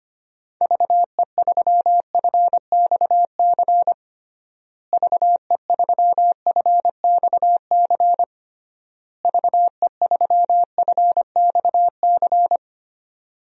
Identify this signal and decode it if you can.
Morse code, heard as a keyed tone, about 25 wpm, 700 Hz. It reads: VE3FXC VE3FXC VE3FXC
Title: DXCC-100 Call Signs - 6 Characters - Repeated 3x - 25wpm